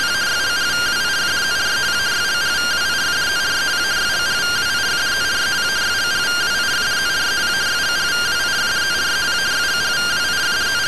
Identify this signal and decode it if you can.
Two-tone teleprinter signal, about 170 Hz shift.